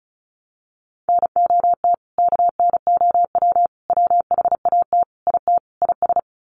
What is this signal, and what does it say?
Morse code, heard as a keyed tone, about 34 wpm, 700 Hz. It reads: NOT KNOW WHAT IT IS
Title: Sets of 5 Words Encore 35wpm